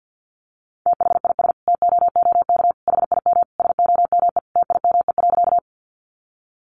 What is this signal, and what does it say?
Morse code, heard as a keyed tone, about 50 wpm, 700 Hz. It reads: THIS TOOK HIM SOME TIME.
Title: Sentences from Top 300 Words 50wpm